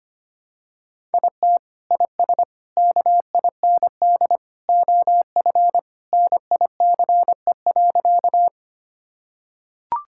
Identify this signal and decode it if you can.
Morse code, heard as a keyed tone, about 25 wpm, 700 Hz. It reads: IT IS KIND OF NICE.
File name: practice / 25wpm / Sentences with 2-Word Phrases - 25wpm